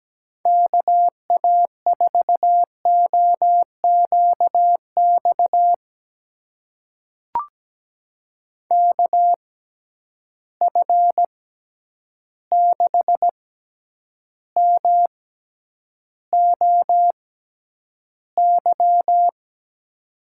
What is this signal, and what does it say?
Morse code, heard as a keyed tone, about 17 wpm, spaced as for 9 wpm, 700 Hz. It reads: KA4OQX KF6MOY
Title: USA Call Signs - 6 Characters - 6x Character Spacing - 17wpm